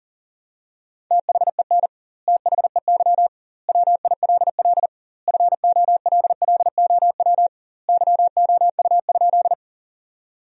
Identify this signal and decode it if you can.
Morse code, heard as a keyed tone, about 40 wpm, 700 Hz. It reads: THEN THEY WILL FOLLOW YOU?
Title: 2x Word-Spacing - Sentences from Top 200 Words - 40wpm